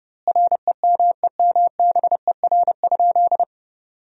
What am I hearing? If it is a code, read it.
Morse code, heard as a keyed tone, about 32 wpm, 700 Hz. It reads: REMEMBER?
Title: 2x Word-Spacing - Sentences from Top 400 Words - 30wpm